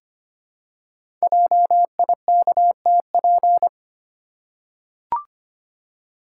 Morse code, heard as a keyed tone, about 25 wpm, 700 Hz. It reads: JIKTP
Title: Code Groups - 5 Letters - 25wpm